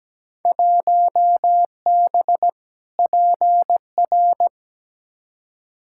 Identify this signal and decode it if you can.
Morse code, heard as a keyed tone, about 17 wpm, 700 Hz. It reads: 1B PR
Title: Field Day Exchanges Encore - 17wpm